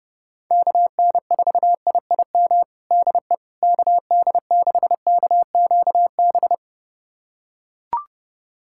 Morse code, heard as a keyed tone, about 30 wpm, 700 Hz. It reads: KN4IIM DE KD6KQB
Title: USA Callsign DE Callsign 30wpm